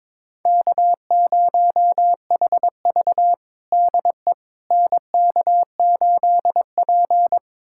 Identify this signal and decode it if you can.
Morse code, heard as a keyed tone, about 22 wpm, 700 Hz. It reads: K0HV DE NK8P
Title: USA Callsign DE Callsign - 4 Characters - 22wpm